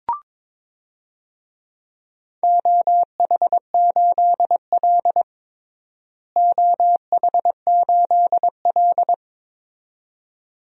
Morse code, heard as a keyed tone, about 21 wpm, 700 Hz. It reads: OH8L OH8L
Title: DXCC-100 Call Signs - 4 Characters - Repeated 2x - 22wpm